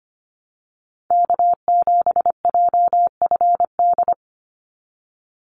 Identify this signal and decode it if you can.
Morse code, heard as a keyed tone, about 25 wpm, 700 Hz. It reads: K7JFD
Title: NA SOTA Chaser Callsigns 25wpm